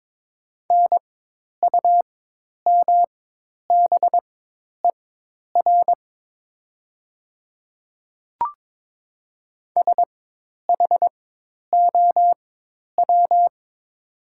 Morse code, heard as a keyed tone, about 22 wpm, spaced as for 10 wpm, 700 Hz. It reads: NUMBER SHOW
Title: 4x Character Spacing - Top 200 Words - 22wpm